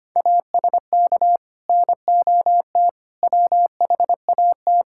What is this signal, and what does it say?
Morse code, heard as a keyed tone, about 25 wpm, 700 Hz. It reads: ASK NOT WHAT
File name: Sets of 9 Words - 25wpm